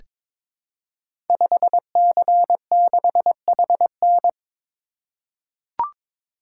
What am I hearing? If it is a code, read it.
Morse code, heard as a keyed tone, about 22 wpm, 700 Hz. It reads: 5C6HN
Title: Code Groups - 5 Letter:Numbers - 22wpm